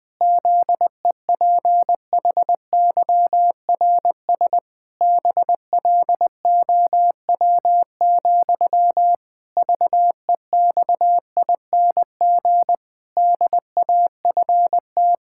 Morse code, heard as a keyed tone, about 20 wpm, 700 Hz. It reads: ZEPHYRS BLOW, VEXING DAFT
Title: Pangram Challenge v2 - 20wpm